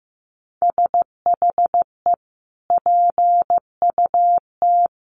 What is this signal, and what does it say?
Morse code, heard as a keyed tone, about 15 wpm, 700 Hz. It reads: SHE PUT
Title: Sentences from Top 1500 Words 15wpm